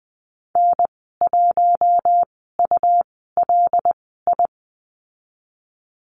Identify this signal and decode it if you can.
Morse code, heard as a keyed tone, about 20 wpm, spaced as for 15 wpm, 700 Hz. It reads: N1ULI
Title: 147 Lesson 24g - Letter y - Call Signs 20wpm